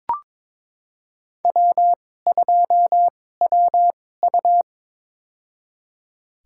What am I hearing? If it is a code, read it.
Morse code, heard as a keyed tone, about 22 wpm, spaced as for 16 wpm, 700 Hz. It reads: W2WU
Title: USA Call Signs - 2x Character Spacing - 22wpm